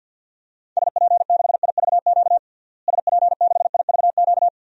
Morse code, heard as a keyed tone, about 49 wpm, 700 Hz. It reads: SP6IVQ SP6IVQ
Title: DXCC-100 Call Signs - 6 Characters - Repeated 2x - 50wpm